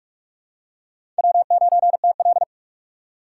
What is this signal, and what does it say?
Morse code, heard as a keyed tone, about 43 wpm, 700 Hz. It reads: W9TL
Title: USA Call Signs - 3x Character Spacing - 45wpm